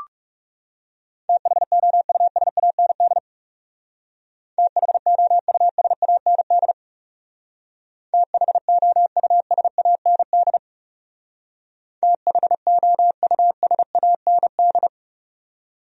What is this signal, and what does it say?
Morse code, heard as a keyed tone, about 36 wpm, 700 Hz. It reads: THOUSAND THOUSAND THOUSAND THOUSAND
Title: Speed-Racing Top 500 Words 30wpm